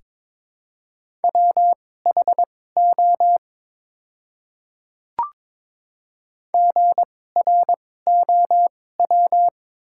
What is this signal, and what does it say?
Morse code, heard as a keyed tone, about 22 wpm, spaced as for 16 wpm, 700 Hz. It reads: WHO GROW
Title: 2x Character Spacing - Top 300 Words - 22wpm